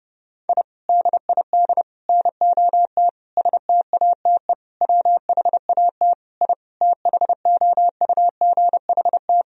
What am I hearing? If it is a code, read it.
Morse code, heard as a keyed tone, about 28 wpm, 700 Hz. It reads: I DID NOT STATE WHAT I THOUGHT
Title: Sentences from Top 1000 Words - Review Part 2 - 30wpm